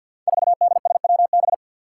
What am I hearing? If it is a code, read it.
Morse code, heard as a keyed tone, about 53 wpm, 700 Hz. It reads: 4DSWB
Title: Five Challenging Letter-Number Combinations 50wpm